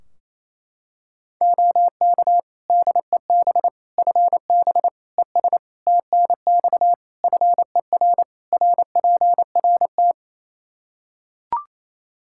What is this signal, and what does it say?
Morse code, heard as a keyed tone, about 28 wpm, 700 Hz. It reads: OK DEB FB ES TNX FER RPRT
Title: Practice QSO - Exchange 2 - Info - 28wpm